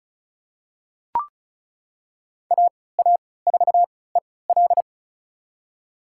Morse code, heard as a keyed tone, about 34 wpm, spaced as for 20 wpm, 690 Hz. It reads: AA4EL
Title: USA Call Signs - 5 Characters - 3x Character Spacing - 35wpm